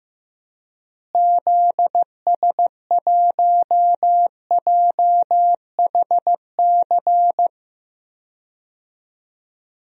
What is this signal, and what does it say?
Morse code, heard as a keyed tone, about 15 wpm, 700 Hz. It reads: ZS1JHC
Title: DXCC-100 Call Signs - 6 Characters - Repeated 3x - 15wpm